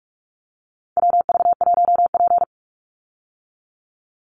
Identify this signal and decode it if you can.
Morse code, heard as a keyed tone, about 44 wpm, 700 Hz. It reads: WV1P
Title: USA Call Signs 45wpm